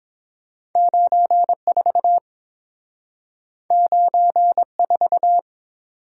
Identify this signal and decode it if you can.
Morse code, heard as a keyed tone, about 30 wpm, 700 Hz. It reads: 94 94